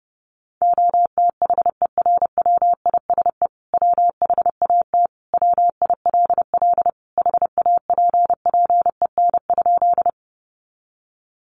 Morse code, heard as a keyed tone, about 30 wpm, 700 Hz. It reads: OTHERWISE WHAT WILL HAPPEN?
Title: Sentences from Top 1200 Words 30wpm